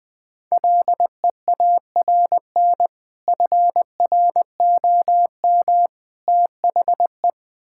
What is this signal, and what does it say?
Morse code, heard as a keyed tone, about 20 wpm, 700 Hz. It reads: LEARN FROM THE
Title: Sentences with 2-Word Phrases - Spoken Beginning - 20wpm